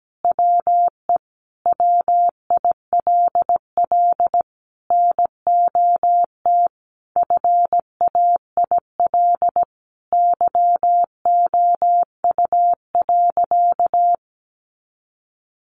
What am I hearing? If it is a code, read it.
Morse code, heard as a keyed tone, about 17 wpm, 700 Hz. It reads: WE WILL NOT FAIL YOU.
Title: Sentences from Top 1100 Words 17wpm